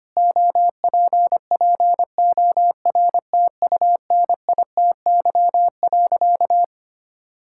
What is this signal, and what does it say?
Morse code, heard as a keyed tone, about 25 wpm, 700 Hz. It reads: OPPORTUNITY.
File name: practice / 25wpm / Sentences with 4-Word Phrases - 25wpm